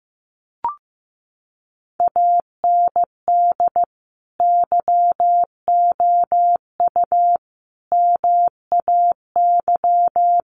Morse code, heard as a keyed tone, about 15 wpm, 700 Hz. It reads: AND YOU MAY
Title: Sentences from Top 1500 Words 15wpm